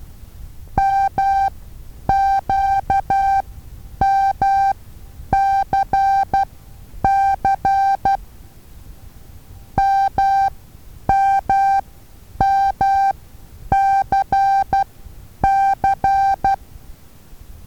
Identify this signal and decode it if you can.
Morse code, heard as a keyed tone, about 12 wpm, spaced as for 9 wpm, 780 Hz. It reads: MQMCC MMMCC
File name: Taviro HA6VA A